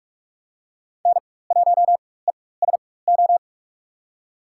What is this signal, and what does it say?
Morse code, heard as a keyed tone, about 43 wpm, spaced as for 21 wpm, 700 Hz. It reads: N1ESO
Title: USA Call Signs - 5 Characters - 4x Character Spacing - 45wpm